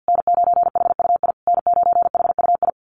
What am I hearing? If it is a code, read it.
Morse code, heard as a keyed tone, about 48 wpm, 700 Hz. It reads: N9HUI N9HUI
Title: USA Call Signs Repeated 3x 50wpm